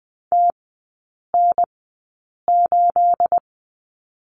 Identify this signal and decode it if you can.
Morse code, heard as a keyed tone, about 20 wpm, spaced as for 8 wpm, 690 Hz. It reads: TN8
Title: 212 Lesson 34e - Number 8 - Three Characters 20wpm